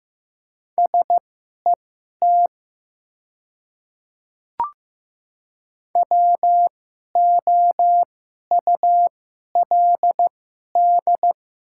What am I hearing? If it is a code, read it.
Morse code, heard as a keyed tone, about 15 wpm, spaced as for 11 wpm, 700 Hz. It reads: SET WOULD
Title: 2x Character Spacing - Top 300 Words - 15wpm